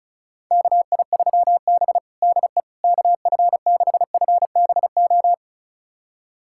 Morse code, heard as a keyed tone, about 36 wpm, 700 Hz. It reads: KI3B DE KF6FBO
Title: USA Callsign DE Callsign 35wpm